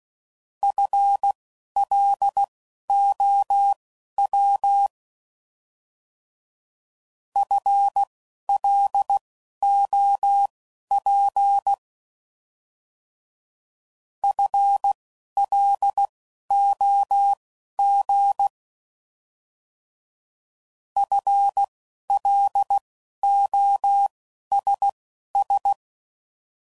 Morse code, heard as a keyed tone, about 16 wpm, spaced as for 11 wpm, 800 Hz. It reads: FLOW FLOP FLOG FLOSS
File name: Prefixes / By First Letter / Fs